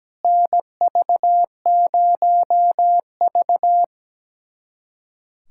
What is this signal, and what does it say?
Morse code, heard as a keyed tone, about 17 wpm, 700 Hz. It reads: NV0V
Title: USA Callsign DE Callsign - 4 Characters - 17wpm